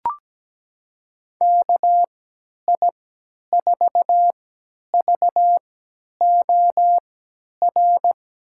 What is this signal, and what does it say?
Morse code, heard as a keyed tone, about 17 wpm, spaced as for 10 wpm, 700 Hz. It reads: KI4VOR
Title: USA Call Signs - 6 Characters - 3x Character Spacing - 17wpm